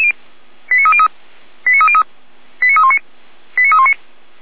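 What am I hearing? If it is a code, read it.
Five-tone selective call (ZVEI), heard as quick runs of five tones, tones of about 70 ms. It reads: 89303 89303 89319 89319
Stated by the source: ZVEI five-tone call 89125, received by SDR, fire brigade Orlamünde (feuerwehr-orlamuende).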